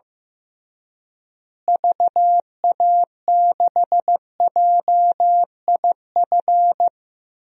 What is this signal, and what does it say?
Morse code, heard as a keyed tone, about 15 wpm, 700 Hz. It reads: VA6JIF
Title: DXCC-100 Call Signs - 6 Characters - Repeated 3x - 15wpm